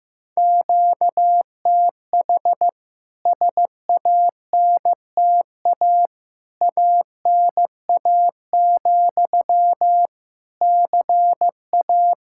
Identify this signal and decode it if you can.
Morse code, heard as a keyed tone, about 15 wpm, 700 Hz. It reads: QTH SANTA ANA, CA
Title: K1USN SST QSO Practice 15wpm